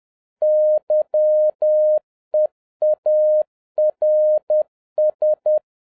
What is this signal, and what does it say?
Morse code, heard as a keyed tone, about 10 wpm, 610 Hz. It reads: YEARS
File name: 2011-10-mp3 / n11-MEETING WITH GOOD FRIENDS 10wpm